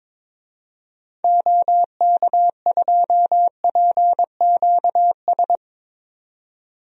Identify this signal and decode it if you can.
Morse code, heard as a keyed tone, about 22 wpm, 700 Hz. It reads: OK2PQS
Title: DXCC-100 Call Signs - 6 Characters - 22wpm